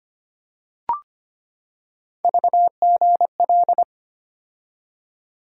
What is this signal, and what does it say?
Morse code, heard as a keyed tone, about 25 wpm, 700 Hz. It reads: VGL
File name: Three Letter-Number Combinations - 25wpm